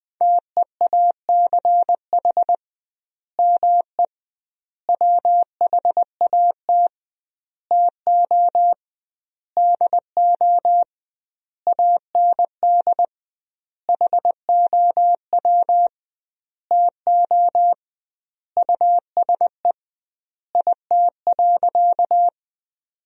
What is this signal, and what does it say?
Morse code, heard as a keyed tone, about 20 wpm, 700 Hz. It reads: TEACH ME WHAT TO DO AND HOW TO USE IT.
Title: 2x Word-Spacing - Sentences from Top 500 Words - 20wpm